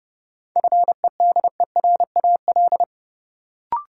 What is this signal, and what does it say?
Morse code, heard as a keyed tone, about 32 wpm, 700 Hz. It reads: FEDERAL
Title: Sets of 3 Words - New Word Drill - 30wpm